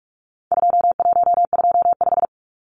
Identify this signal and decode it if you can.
Morse code, heard as a keyed tone, about 44 wpm, 700 Hz. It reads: 2125
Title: Numbers - 4-Digit 45wpm